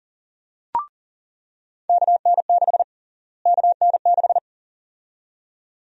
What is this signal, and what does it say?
Morse code, heard as a keyed tone, about 38 wpm, 700 Hz. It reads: KN6 KN6
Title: USA Call Sign Prefixes - Repeated 2x - 40wpm